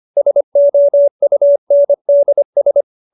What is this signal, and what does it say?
Morse code, heard as a keyed tone, about 25 wpm, 570 Hz. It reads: SOUNDS